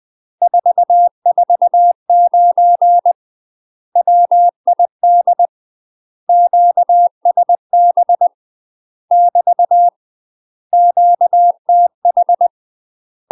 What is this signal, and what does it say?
Morse code, heard as a keyed tone, about 20 wpm, 700 Hz. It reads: 449 WID QSB <BT> QTH
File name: UK MASTER QSOs TRAINING / 20WPM / UK MASTER QSOs 0012